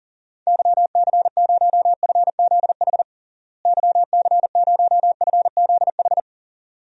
Morse code, heard as a keyed tone, about 40 wpm, 700 Hz. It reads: YC0FZH YC0FZH
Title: DXCC-100 Call Signs - 6 Characters - Repeated 2x - 40wpm